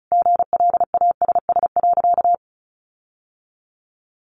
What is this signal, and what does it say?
Morse code, heard as a keyed tone, about 34 wpm, 700 Hz. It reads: GLASS.
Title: Sentences from Top 1500 Words 35wpm